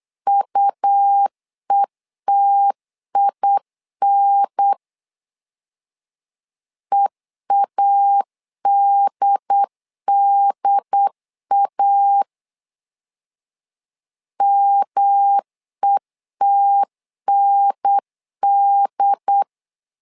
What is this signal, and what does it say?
Morse code, heard as a keyed tone, about 8 wpm, 800 Hz. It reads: UETIN EADDA METND
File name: LEZIONI / 3 MWRO / 3 M 8wpm 2300